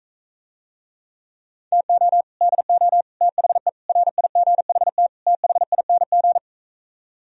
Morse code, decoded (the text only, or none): TO DO THE RIGHT THING